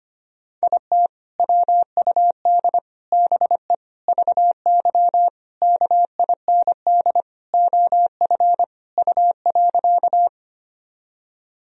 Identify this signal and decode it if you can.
Morse code, heard as a keyed tone, about 25 wpm, 700 Hz. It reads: IT WUD BE VY KIND OF U.
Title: Sentences from Top 200 Words and Abbreviations 25wpm